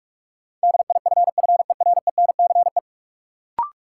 Morse code, heard as a keyed tone, about 46 wpm, 700 Hz. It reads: DIFFERENCE
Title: Sentences from Top 800 Words - New Word Drill - 45wpm